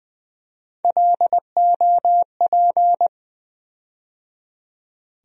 Morse code, heard as a keyed tone, about 20 wpm, 700 Hz. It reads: LOP
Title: USA Call Sign Postfixes - 3 Letters 20wpm